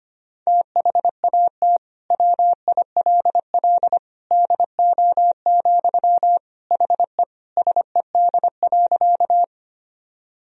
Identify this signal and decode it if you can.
Morse code, heard as a keyed tone, about 25 wpm, 700 Hz. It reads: THAT WILL DO, HE SED.
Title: Sentences from Top 100 Words and Abbreviations 25wpm